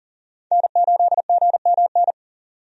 Morse code, heard as a keyed tone, about 40 wpm, 700 Hz. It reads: N8GMN
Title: USA Call Signs - 5 Characters - 40wpm